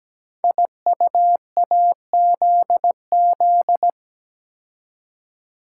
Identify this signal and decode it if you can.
Morse code, heard as a keyed tone, about 17 wpm, 700 Hz. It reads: IUAZZ
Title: Code Groups - 5 Letters - 17wpm